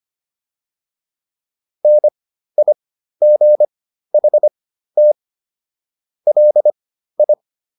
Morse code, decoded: NIGHT LI

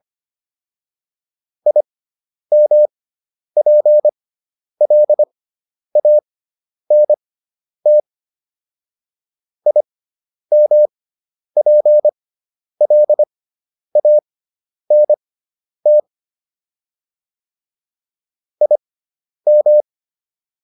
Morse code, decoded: IMPLANT IMPLANT IM

